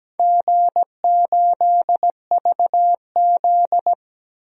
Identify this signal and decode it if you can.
Morse code, heard as a keyed tone, about 17 wpm, 700 Hz. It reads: G8VZ
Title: DXCC-100 Call Signs - 4 Characters - Repeated 3x - 17wpm